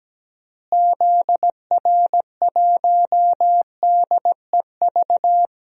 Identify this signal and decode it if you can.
Morse code, heard as a keyed tone, about 17 wpm, 700 Hz. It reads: ZR1DEV
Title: DXCC-100 Call Signs - 6 Characters - Repeated 3x - 17wpm